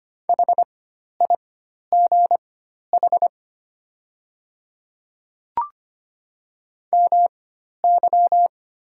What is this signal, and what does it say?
Morse code, heard as a keyed tone, about 25 wpm, spaced as for 12 wpm, 700 Hz. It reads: HIGH MY
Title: 4x Character Spacing - Top 200 Words - 25wpm